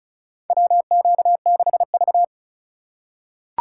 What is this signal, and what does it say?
Morse code, heard as a keyed tone, about 34 wpm, 700 Hz. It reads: WQ6V
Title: USA Call Signs - 4 Characters - 4x Character Spacing - 35wpm